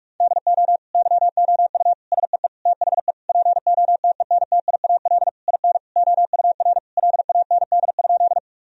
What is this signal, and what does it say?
Morse code, heard as a keyed tone, about 46 wpm, 700 Hz. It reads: DO YOU SEE THE POTENTIAL IN OUR LAND?
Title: Sentences from Top 1000 Words - Review Part 4 - 45wpm